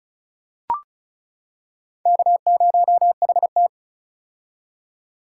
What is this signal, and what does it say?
Morse code, heard as a keyed tone, about 34 wpm, 700 Hz. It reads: K0HT
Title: K1USN SST QSO Practice - 35wpm